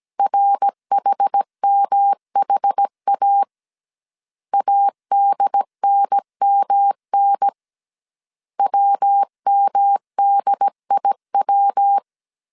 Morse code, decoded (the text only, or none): RHMHA ADNMN WMDIW